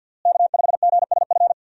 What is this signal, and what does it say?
Morse code, heard as a keyed tone, about 52 wpm, 700 Hz. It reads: K5GSF
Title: USA Call Signs - 5 Characters - Repeated 3x - 50wpm